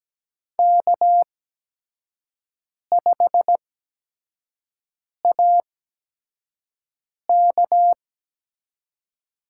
Morse code, heard as a keyed tone, about 17 wpm, spaced as for 5 wpm, 700 Hz. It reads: K5AK
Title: USA Call Signs - 5 Characters - 8x Character Spacing - 17wpm